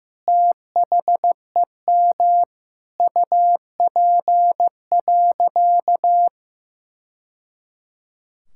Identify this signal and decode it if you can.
Morse code, heard as a keyed tone, about 15 wpm, 700 Hz. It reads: THEM UP.
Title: Sentences from Top 1000 Words - Review Part 6 - 15wpm